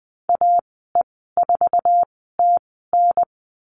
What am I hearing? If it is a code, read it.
Morse code, heard as a keyed tone, about 20 wpm, spaced as for 14 wpm, 700 Hz. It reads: AE4TN
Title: 055 Lesson 10g - Letter r - Call Signs 20wpm